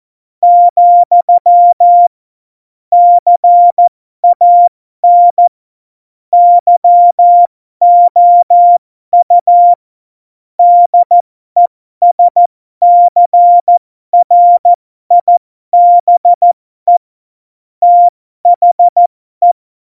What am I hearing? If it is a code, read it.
Morse code, heard as a keyed tone, about 14 wpm, spaced as for 12 wpm, 700 Hz. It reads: , CAN YOU DESCRIBE THE
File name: Morsekurs / Witze